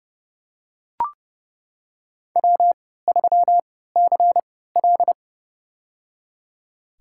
Morse code, heard as a keyed tone, about 29 wpm, spaced as for 17 wpm, 700 Hz. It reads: W3CL
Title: USA Call Signs - 4 Characters - 3x Character Spacing - 30wpm